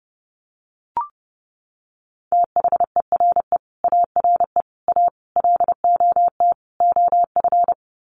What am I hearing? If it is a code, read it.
Morse code, heard as a keyed tone, about 30 wpm, 700 Hz. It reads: THERE ARE A LOT OF
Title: Sets of 5 Words 30wpm